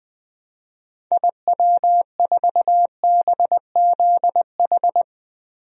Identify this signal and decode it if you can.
Morse code, heard as a keyed tone, about 20 wpm, 700 Hz. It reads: IW4BZH
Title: DXCC-100 Call Signs - 6 Characters - 20wpm